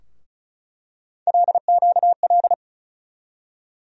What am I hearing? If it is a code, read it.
Morse code, heard as a keyed tone, about 36 wpm, 700 Hz. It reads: LQL
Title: USA Call Sign Postfixes - 3 Letters 35wpm